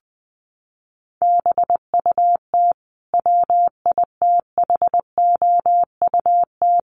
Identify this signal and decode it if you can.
Morse code, heard as a keyed tone, about 20 wpm, 700 Hz. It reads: BUT WITHOUT
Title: Sentences from Top 900 Words 20wpm